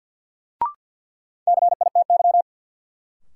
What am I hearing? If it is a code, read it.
Morse code, heard as a keyed tone, about 53 wpm, 700 Hz. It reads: CITY
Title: Sentences from Top 300 Words - New Word Drill - 50wpm